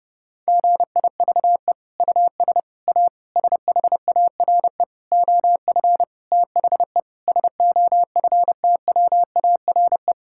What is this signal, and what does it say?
Morse code, heard as a keyed tone, about 30 wpm, 700 Hz. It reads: GIVE US A SHARE OF THE SOFTWARE
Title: Sentences from Top 1000 Words - Review Part 6 - 30wpm